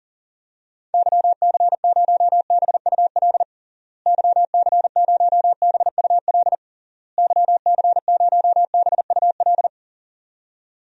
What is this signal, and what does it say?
Morse code, heard as a keyed tone, about 40 wpm, 700 Hz. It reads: YC0BUL YC0BUL YC0BUL
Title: DXCC-100 Call Signs - 6 Characters - Repeated 3x - 40wpm